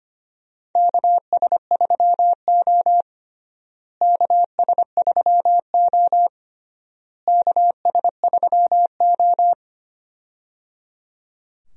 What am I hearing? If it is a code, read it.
Morse code, heard as a keyed tone, about 25 wpm, 700 Hz. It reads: KS3O KS3O KS3O
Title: USA Call Signs - 4 Characters - Repeated 3x - 25wpm